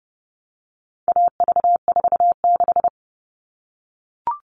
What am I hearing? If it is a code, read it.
Morse code, heard as a keyed tone, about 30 wpm, 700 Hz. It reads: AV46
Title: All Cut Numbers - 4-Digit 30wpm